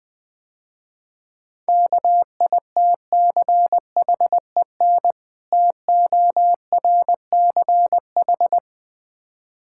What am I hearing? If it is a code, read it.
Morse code, heard as a keyed tone, about 21 wpm, 700 Hz. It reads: KITCHEN TORCH
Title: Difficult Words - Speed-Racing with Hints - 20wpm